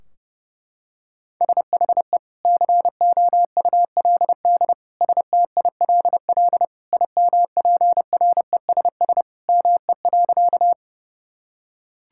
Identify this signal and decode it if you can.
Morse code, heard as a keyed tone, about 30 wpm, 700 Hz. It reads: SHE COULD STILL IMPRESS ME.
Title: Sentences from Top 1600 Words 30wpm